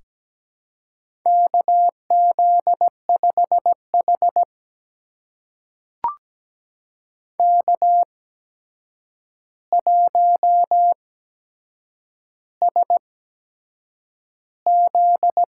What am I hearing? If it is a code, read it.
Morse code, heard as a keyed tone, about 17 wpm, spaced as for 7 wpm, 700 Hz. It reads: KZ5H K1SZ
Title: USA Call Signs - 4 Characters - 8x Character Spacing - 17wpm